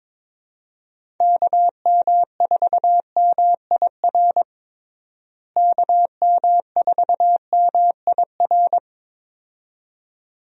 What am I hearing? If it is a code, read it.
Morse code, heard as a keyed tone, about 22 wpm, 700 Hz. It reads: KM4MIR KM4MIR
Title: USA Call Signs - 6 Characters - Repeated 2x - 22wpm